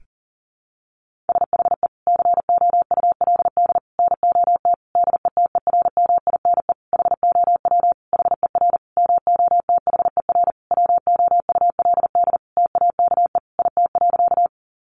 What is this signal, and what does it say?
Morse code, heard as a keyed tone, about 40 wpm, 700 Hz. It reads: SHE COULD NOT DETERMINE HOW HER MOTHER WOULD TAKE IT.
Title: Sentences from Top 700 Words 40wpm